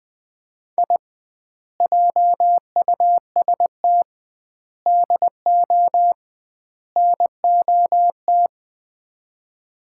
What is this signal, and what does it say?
Morse code, heard as a keyed tone, about 20 wpm, 700 Hz. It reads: I JUST DO NOT
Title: 2x Word-Spacing - Sets of 4 Words - 20wpm